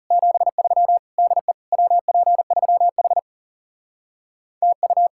Morse code, heard as a keyed tone, about 37 wpm, 700 Hz. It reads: 73 DE WP3H TU
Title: Speed-Racing Common QSO Phrases - 30wpm